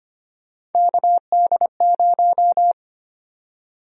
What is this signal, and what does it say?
Morse code, heard as a keyed tone, about 25 wpm, 700 Hz. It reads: KD0
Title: USA Call Sign Prefixes - 25wpm